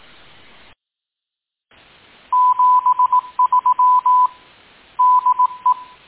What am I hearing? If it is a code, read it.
Morse code, heard as a keyed tone, about 18 wpm, 1000 Hz. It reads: 73 DE
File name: tin-upgrade / QSO 2020 09 08